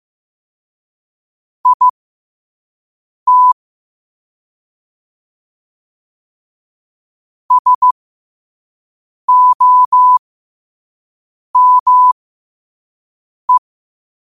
Morse code, heard as a keyed tone, about 15 wpm, spaced as for 5 wpm, 1000 Hz. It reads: IT SOME
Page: New quotes every day in morse code at 5 Words per minute.